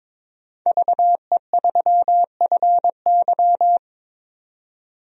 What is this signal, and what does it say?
Morse code, heard as a keyed tone, about 22 wpm, 700 Hz. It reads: VE3FY
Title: DXCC-100 Call Signs - 5 Characters - 22wpm